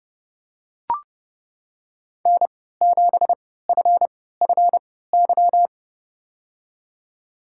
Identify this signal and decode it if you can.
Morse code, heard as a keyed tone, about 30 wpm, spaced as for 17 wpm, 700 Hz. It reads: N7FFY
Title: USA Call Signs - 5 Characters - 3x Character Spacing - 30wpm